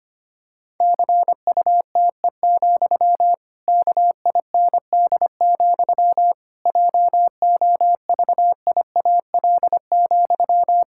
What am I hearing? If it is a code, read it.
Morse code, heard as a keyed tone, about 25 wpm, 700 Hz. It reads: CUTE, KIND, JOVIAL,
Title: Pangram Challenge v2 - 25wpm